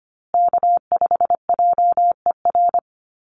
Morse code, heard as a keyed tone, about 25 wpm, 700 Hz. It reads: K5JER
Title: USA Call Signs Repeated 3x 25wpm